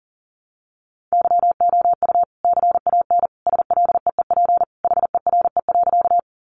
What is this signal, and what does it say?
Morse code, heard as a keyed tone, about 40 wpm, 700 Hz. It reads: YOU CAN SLEEP HERE.
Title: Sentences from Top 500 Words 40wpm